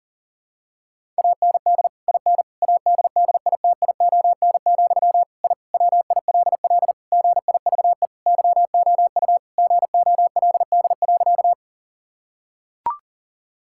Morse code, heard as a keyed tone, about 40 wpm, 700 Hz. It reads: AND IN ADDITION, I WILL GIVE YOU GOLD.
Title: Sentences from Top 1000 Words - Review Part 6 - 40wpm